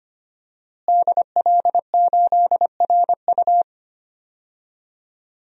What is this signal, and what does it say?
Morse code, heard as a keyed tone, about 25 wpm, 700 Hz. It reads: DL8RU
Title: DXCC-100 Call Signs - 5 Characters - Repeated 2x - 25wpm